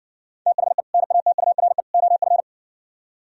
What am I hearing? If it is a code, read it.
Morse code, heard as a keyed tone, about 60 wpm, 700 Hz. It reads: THE NATURE OF